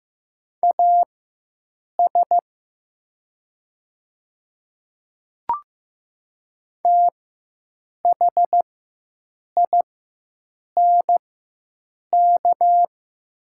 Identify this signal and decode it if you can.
Morse code, heard as a keyed tone, about 15 wpm, spaced as for 7 wpm, 700 Hz. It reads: AS THINK
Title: 4x Character Spacing - Top 300 Words - 15wpm